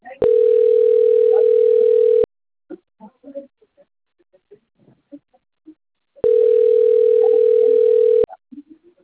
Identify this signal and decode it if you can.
Ringback tone (North America), 440 + 480 Hz, with speech.
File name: out- - - - - .33032